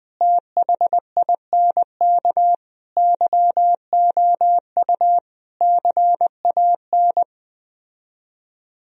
Morse code, heard as a keyed tone, about 20 wpm, 700 Hz. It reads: THINK YOU CAN